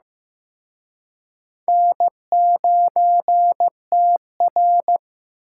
Morse code, heard as a keyed tone, about 18 wpm, 700 Hz. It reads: N9TR
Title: Speed Racing - USA 1x2 Call Signs 15wpm